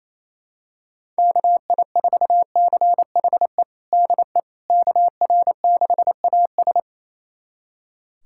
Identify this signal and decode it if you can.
Morse code, heard as a keyed tone, about 27 wpm, 700 Hz. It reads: KI4CHE DE KR6AS
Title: USA Callsign DE Callsign 28wpm